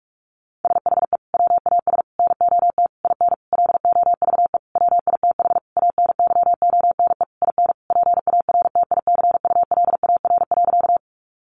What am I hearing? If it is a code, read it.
Morse code, heard as a keyed tone, about 44 wpm, 700 Hz. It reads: SHE WAS NOT IN LOVE WITH ANYONE IN PARTICULAR.
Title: Sentences from Top 900 Words 45wpm